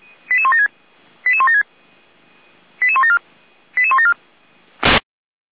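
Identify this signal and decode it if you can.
Five-tone selective call (ZVEI), heard as quick runs of five tones, tones of about 70 ms. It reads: 88176 88176 88174 88174
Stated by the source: Five-tone ZVEI selective call 88314, received by SDR.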